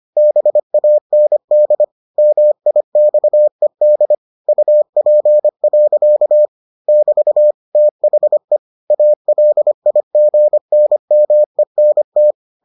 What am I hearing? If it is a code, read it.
Morse code, heard as a keyed tone, about 25 wpm, 600 Hz. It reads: BAND MIXED UP. <BT> THE ALIGNMENT